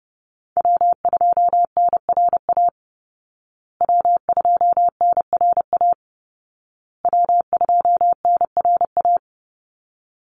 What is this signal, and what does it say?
Morse code, heard as a keyed tone, about 30 wpm, 700 Hz. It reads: W2NRA W2NRA W2NRA
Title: Call Signs Repeated 2x - 30wpm